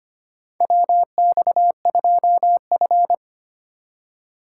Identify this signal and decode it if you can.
Morse code, heard as a keyed tone, about 25 wpm, 700 Hz. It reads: WX2F
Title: USA Call Signs - 4 Characters - 2x Character Spacing - 25wpm